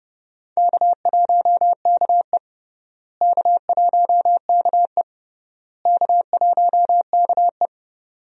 Morse code, heard as a keyed tone, about 30 wpm, 700 Hz. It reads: K1KE K1KE K1KE
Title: USA Call Signs - 4 Characters - Repeated 3x - 30wpm